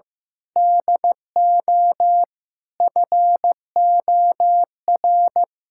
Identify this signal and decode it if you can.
Morse code, heard as a keyed tone, about 16 wpm, 700 Hz. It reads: DO FOR
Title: Sentences from Top 1600 Words 15wpm